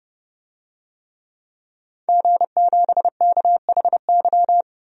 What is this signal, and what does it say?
Morse code, heard as a keyed tone, about 30 wpm, 700 Hz. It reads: G7KHY
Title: DXCC-100 Call Signs - 5 Characters - 30wpm